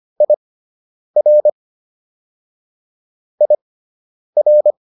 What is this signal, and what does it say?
Morse code, heard as a keyed tone, about 25 wpm, spaced as for 9 wpm, 600 Hz. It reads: IR IR